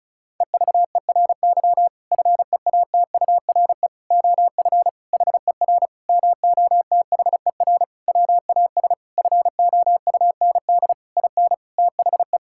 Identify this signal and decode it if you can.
Morse code, heard as a keyed tone, about 34 wpm, 700 Hz. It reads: EVERY FEATURE OF HER MOTHER WAS FOUND IN THE
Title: Sentences from Top 1000 Words - Review Part 5 - 35wpm